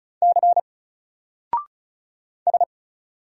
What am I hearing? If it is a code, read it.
Morse code, heard as a keyed tone, about 36 wpm, 700 Hz. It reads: C S
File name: Single-Letter-Number-35wpm